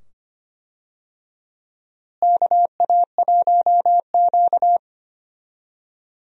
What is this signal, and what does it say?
Morse code, heard as a keyed tone, about 25 wpm, 700 Hz. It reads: KA1Q
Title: USA Call Signs - 4 Characters - Repeated 2x - 25wpm